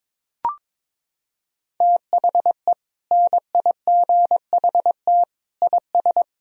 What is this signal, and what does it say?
Morse code, heard as a keyed tone, about 22 wpm, 700 Hz. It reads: THE NIGHT IS
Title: Sentences from Top 500 Words and Abbreviations 22wpm